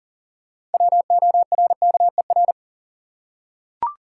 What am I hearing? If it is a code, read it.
Morse code, heard as a keyed tone, about 40 wpm, 700 Hz. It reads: WORKER
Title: Six Letter Words - 40wpm